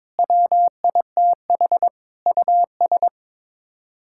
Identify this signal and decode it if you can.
Morse code, heard as a keyed tone, about 22 wpm, 700 Hz. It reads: WITH US
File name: Sets of 9 Words - 22wpm